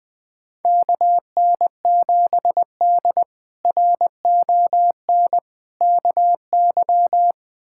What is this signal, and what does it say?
Morse code, heard as a keyed tone, about 20 wpm, 700 Hz. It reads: KN7D RON KY
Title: K1USN SST QSO Practice - 20wpm